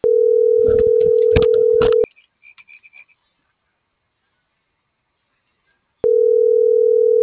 Ringback tone (North America), 440 + 480 Hz.
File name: CENTRAL TERRESTRE / out cris.f